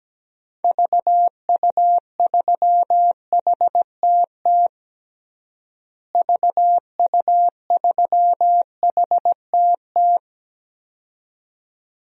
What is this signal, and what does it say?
Morse code, heard as a keyed tone, about 17 wpm, 700 Hz. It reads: VU3HTT VU3HTT
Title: DXCC-100 Call Signs - 6 Characters - Repeated 2x - 17wpm